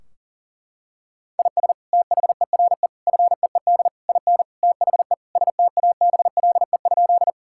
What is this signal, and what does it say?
Morse code, heard as a keyed tone, about 40 wpm, 700 Hz. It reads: IS THERE FEED IN THE STABLE?
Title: Sentences from Top 1600 Words 40wpm